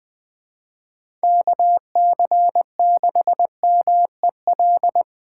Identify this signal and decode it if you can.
Morse code, heard as a keyed tone, about 20 wpm, 700 Hz. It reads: KC6MEL
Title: USA Call Signs - 4x Character Spacing - 20wpm